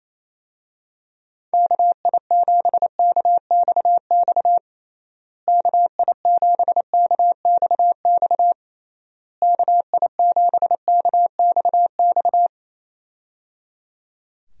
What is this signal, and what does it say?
Morse code, heard as a keyed tone, about 28 wpm, 700 Hz. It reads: KI7KXX KI7KXX KI7KXX
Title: USA Call Signs - 6 Characters - Repeated 3x - 28wpm